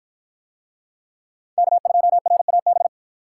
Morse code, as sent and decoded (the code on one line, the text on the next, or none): -.- ..--- .-. .- -...
K2RAB